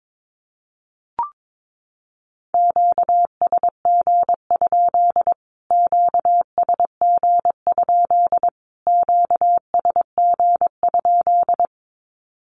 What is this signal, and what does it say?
Morse code, heard as a keyed tone, about 22 wpm, 700 Hz. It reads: QSG? QSG? QSG?
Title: Q-Codes Repeated 3x 22wpm